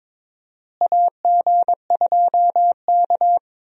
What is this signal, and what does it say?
Morse code, heard as a keyed tone, about 22 wpm, 700 Hz. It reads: AG2K
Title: USA Call Signs - 4 Characters - Repeated 3x - 22wpm